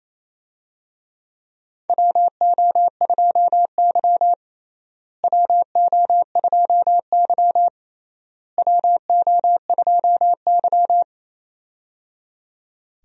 Morse code, heard as a keyed tone, about 28 wpm, 700 Hz. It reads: WO2Y WO2Y WO2Y
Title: USA Call Signs - 4 Characters - Repeated 3x - 28wpm